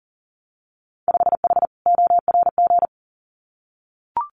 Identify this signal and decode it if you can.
Morse code, heard as a keyed tone, about 43 wpm, 700 Hz. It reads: 5H ORG
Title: Winter Field Day Exchanges Encore 40wpm